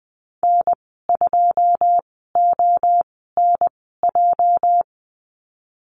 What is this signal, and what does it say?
Morse code, heard as a keyed tone, about 20 wpm, spaced as for 15 wpm, 700 Hz. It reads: N2ONJ
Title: 231 Lesson 37g - Letter x - Call Signs 20wpm